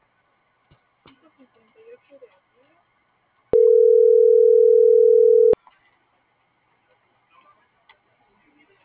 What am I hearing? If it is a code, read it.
Ringback tone (North America), 440 + 480 Hz.